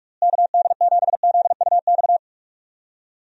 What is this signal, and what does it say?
Morse code, heard as a keyed tone, about 44 wpm, 700 Hz. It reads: KD7ZUX
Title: USA Call Signs - 6 Characters - Repeated 2x - 45wpm